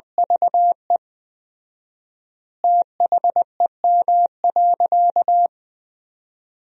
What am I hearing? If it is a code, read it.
Morse code, heard as a keyed tone, about 20 wpm, 700 Hz. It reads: VE THEM.
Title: 4x Word-Spacing - Sentences from Top 400 Words - 20wpm